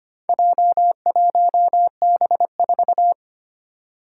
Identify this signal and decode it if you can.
Morse code, heard as a keyed tone, about 25 wpm, 700 Hz. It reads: J1B4
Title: Four Challenging Letter-Number Combinations 25wpm